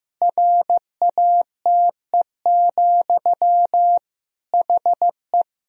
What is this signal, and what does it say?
Morse code, heard as a keyed tone, about 15 wpm, 700 Hz. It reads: RATE, HE
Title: Sentences from Top 1000 Words - Review Part 4 - 15wpm